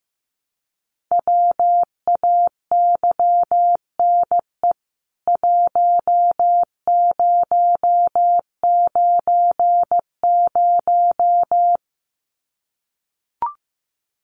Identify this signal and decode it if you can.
Morse code, heard as a keyed tone, about 15 wpm, 700 Hz. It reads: WAYNE 1090
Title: CWT Exchange 15wpm